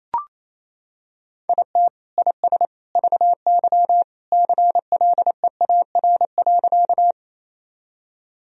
Morse code, heard as a keyed tone, about 27 wpm, 700 Hz. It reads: IT IS VY CLEAR.
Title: Sentences from Top 500 Words and Abbreviations 28wpm